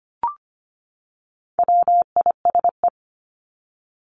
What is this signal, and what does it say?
Morse code, heard as a keyed tone, about 25 wpm, 700 Hz. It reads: WISE
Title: Top 1501-1750 Words 25wpm